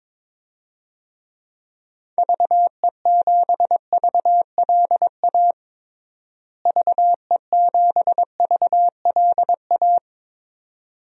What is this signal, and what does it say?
Morse code, heard as a keyed tone, about 22 wpm, 700 Hz. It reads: VE7VLA VE7VLA
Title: DXCC-100 Call Signs - 6 Characters - Repeated 2x - 22wpm